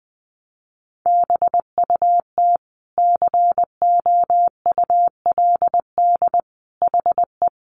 Morse code, heard as a keyed tone, about 20 wpm, 700 Hz. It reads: BUT COULD HE
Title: Sentences from Top 1200 Words 20wpm